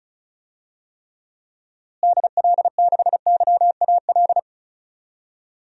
Morse code, heard as a keyed tone, about 36 wpm, 700 Hz. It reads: DL6YAL
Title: DXCC-100 Call Signs - 6 Characters - Repeated 3x - 35wpm